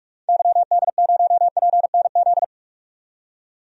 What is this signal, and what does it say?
Morse code, heard as a keyed tone, about 46 wpm, 700 Hz. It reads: YD0PNZ
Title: DXCC-100 Call Signs - 6 Characters - 45wpm